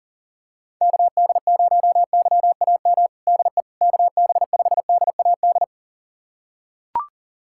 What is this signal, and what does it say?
Morse code, heard as a keyed tone, about 40 wpm, 690 Hz. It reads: KD0YAM DE KB5DAD
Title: USA Callsign DE Callsign - 6 Characters - 40wpm